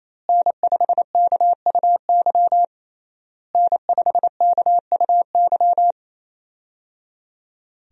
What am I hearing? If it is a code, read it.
Morse code, heard as a keyed tone, about 28 wpm, 700 Hz. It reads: N5KUY N5KUY
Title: USA Call Signs - 5 Characters - Repeated 3x - 28wpm